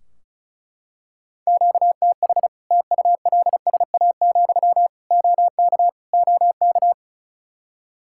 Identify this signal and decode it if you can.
Morse code, heard as a keyed tone, about 34 wpm, 700 Hz. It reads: QTH TULSA, OK OK
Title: Practice QSO - Exchange 1 - Beginning - 35wpm